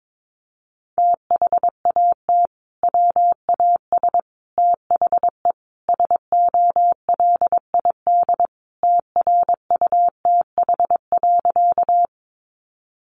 Morse code, heard as a keyed tone, about 22 wpm, 700 Hz. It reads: THAT WAS THE SOLID TRUTH.
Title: Sentences from Top 1500 Words 22wpm